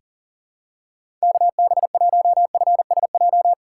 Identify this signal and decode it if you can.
Morse code, heard as a keyed tone, about 40 wpm, 700 Hz. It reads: KB1FSJ
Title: USA Call Signs - 6 Characters - 4x Character Spacing - 40wpm